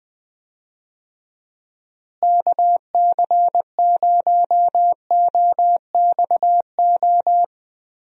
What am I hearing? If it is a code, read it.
Morse code, heard as a keyed tone, about 20 wpm, 700 Hz. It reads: KC0OXO
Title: USA Call Signs - 6 Characters - Repeated 2x - 20wpm